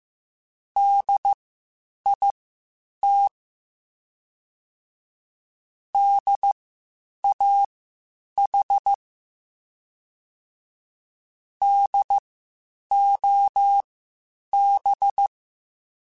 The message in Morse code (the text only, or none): DIT DAH DOB